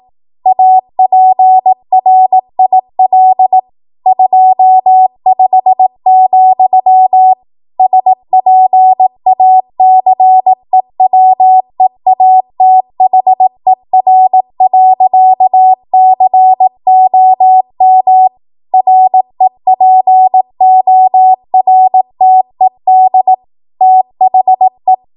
Morse code, heard as a keyed tone, about 18 wpm, 750 Hz. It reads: APRIL 25, SPACEWEATHER.COM REPORTED THE
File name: Morse / 18WPM